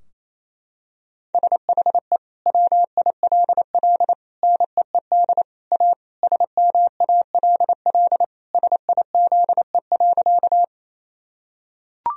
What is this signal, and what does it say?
Morse code, heard as a keyed tone, about 28 wpm, 700 Hz. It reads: SHE WILL NEED A SMALL SIZE.
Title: Sentences from Top 1000 Words - Review Part 2 - 28wpm